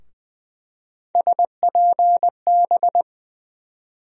Morse code, heard as a keyed tone, about 20 wpm, 700 Hz. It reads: SPB
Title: USA Call Sign Postfixes - 3 Letters 20wpm